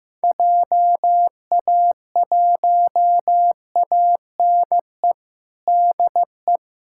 Morse code, heard as a keyed tone, about 15 wpm, 700 Hz. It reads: JA1ANE DE
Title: USA Callsign DE Callsign 15wpm